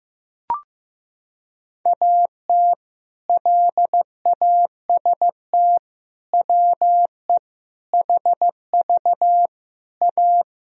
Morse code, heard as a keyed tone, about 15 wpm, 700 Hz. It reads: AT LAST WE HV A
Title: Sentences from Top 300 Words and Abbreviations 15wpm